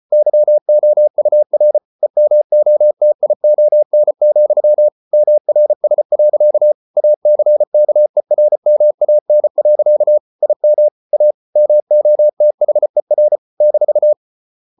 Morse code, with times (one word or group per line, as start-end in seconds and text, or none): 0.1-1.8 YOUR
2.0-4.9 EMOTION,
5.1-6.7 MRS.
7.0-10.2 ACKERMAN.
10.4-10.9 IM
11.1-11.3 A
11.6-13.4 MOTHER
13.6-14.1 -